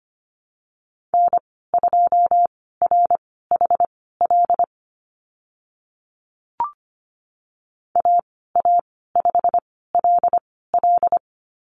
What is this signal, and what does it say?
Morse code, heard as a keyed tone, about 25 wpm, spaced as for 16 wpm, 690 Hz. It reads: N2RHL AA5LL
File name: Lesson 15g - Number 5 - Call Signs 25wpm